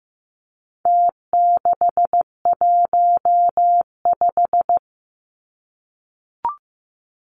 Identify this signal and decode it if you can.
Morse code, heard as a keyed tone, about 15 wpm, 700 Hz. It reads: T615
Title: Numbers - Contest Style - 4-Digit 15wpm